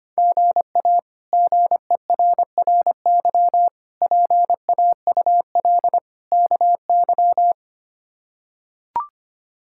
Morse code, heard as a keyed tone, about 25 wpm, 700 Hz. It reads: GA GERRY PAUL KY
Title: K1USN SST QSO Practice - 25wpm